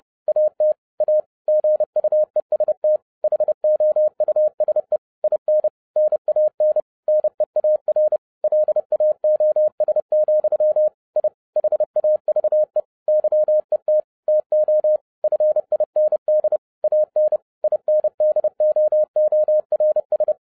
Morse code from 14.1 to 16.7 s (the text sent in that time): TO FIND